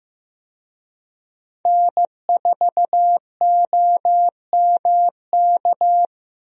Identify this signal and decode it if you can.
Morse code, heard as a keyed tone, about 16 wpm, 700 Hz. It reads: N4OMK
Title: USA Call Signs - 5 Characters - Repeated 3x - 15wpm